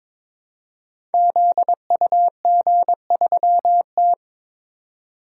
Morse code, heard as a keyed tone, about 22 wpm, 700 Hz. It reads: ZUG3T
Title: Code Groups - 5 Letter:Numbers - 22wpm